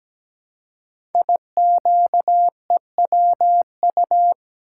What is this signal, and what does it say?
Morse code, heard as a keyed tone, about 17 wpm, 700 Hz. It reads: IQEWU
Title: Code Groups - 5 Letters - 17wpm